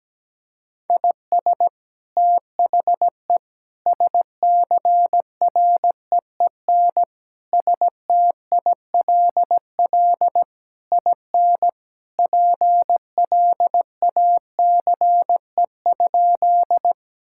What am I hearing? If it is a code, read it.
Morse code, heard as a keyed tone, about 17 wpm, 700 Hz. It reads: IS THE SCREEN STILL IN PLACE?
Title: Sentences from Top 1000 Words - Review Part 6 - 17wpm